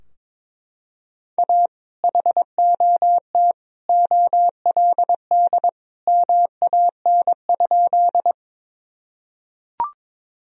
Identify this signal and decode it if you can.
Morse code, heard as a keyed tone, about 21 wpm, 700 Hz. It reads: A HOT OLD MAN?
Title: Sentences from Top 1000 Words - Review Part 3 - 22wpm